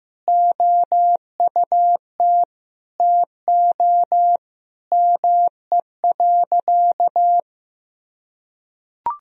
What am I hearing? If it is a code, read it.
Morse code, heard as a keyed tone, about 15 wpm, 700 Hz. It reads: OUT TO ME.
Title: Sentences from Top 1000 Words - Review Part 2 - 15wpm